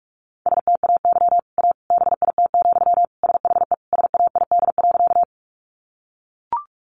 Morse code, heard as a keyed tone, about 44 wpm, 700 Hz. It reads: STAY A BIT, SHE SAID.
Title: Sentences from Top 900 Words 45wpm